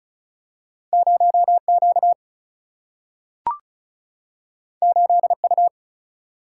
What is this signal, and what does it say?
Morse code, heard as a keyed tone, about 36 wpm, 700 Hz. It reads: 0Q 8U
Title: Single-Letter Single-Number 35wpm